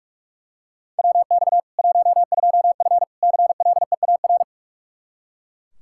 Morse code, heard as a keyed tone, about 42 wpm, 700 Hz. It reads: WX 12F CLEAR
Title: Practice QSO - Exchange 2 - Info - 45wpm